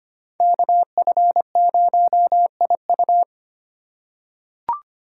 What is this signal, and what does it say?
Morse code, heard as a keyed tone, about 25 wpm, 700 Hz. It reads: KF0IU
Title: USA Call Signs - 5 Characters - 2x Character Spacing - 25wpm